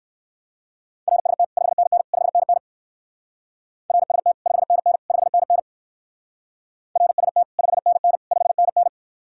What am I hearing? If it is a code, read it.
Morse code, heard as a keyed tone, about 62 wpm, 700 Hz. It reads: RST 5NN 5NN RST 5NN 5NN RST 5NN 5NN